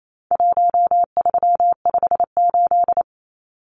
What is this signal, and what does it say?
Morse code, heard as a keyed tone, about 28 wpm, 700 Hz. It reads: 1358
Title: Numbers - Contest Style - 4-Digit 28wpm